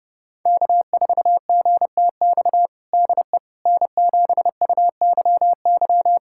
Morse code, heard as a keyed tone, about 32 wpm, 700 Hz. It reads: K4GTX DE N7UYY
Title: USA Callsign DE Callsign 30wpm